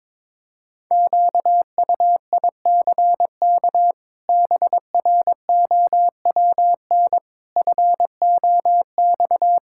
Morse code, heard as a keyed tone, about 21 wpm, 700 Hz. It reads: QUICK BROWN FOX
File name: practice / 22wpm / Sets of 8 Words - 22wpm